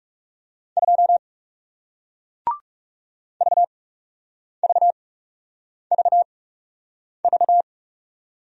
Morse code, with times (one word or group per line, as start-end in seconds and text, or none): 0.8-1.2 2
3.4-7.6 VVVV